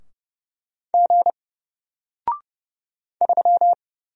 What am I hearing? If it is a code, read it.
Morse code, heard as a keyed tone, about 31 wpm, 700 Hz. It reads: G 3
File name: Single-Letter-Number-30wpm